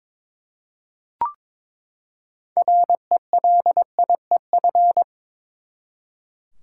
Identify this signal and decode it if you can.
Morse code, heard as a keyed tone, about 22 wpm, 700 Hz. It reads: RELIEF
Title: Six Letter Words - 22wpm